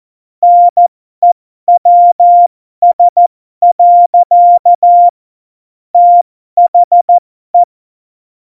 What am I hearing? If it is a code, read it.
Morse code, heard as a keyed tone, about 14 wpm, spaced as for 12 wpm, 700 Hz. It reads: NEWS. THE
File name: Morsekurs / Witze